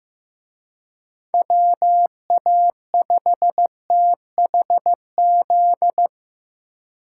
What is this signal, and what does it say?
Morse code, heard as a keyed tone, about 18 wpm, 700 Hz. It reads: WA5THZ
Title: Speed Racing - USA 2x3 Call Signs 15wpm